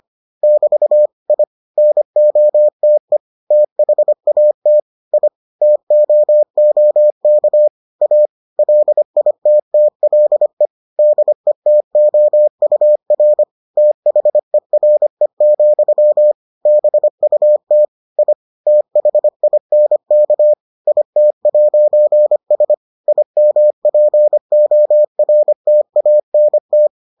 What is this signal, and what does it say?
Morse code, heard as a keyed tone, about 25 wpm, 600 Hz. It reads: <BT> I NOTE THAT I TOOK A LITTLE DETOUR THERE, BUT I THINK IT'S IMPORTANT